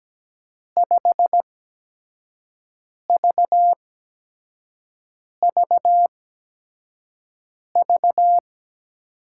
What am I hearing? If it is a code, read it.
Morse code, heard as a keyed tone, about 17 wpm, spaced as for 5 wpm, 700 Hz. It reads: 5VVV
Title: USA Call Signs - 5 Characters - 8x Character Spacing - 17wpm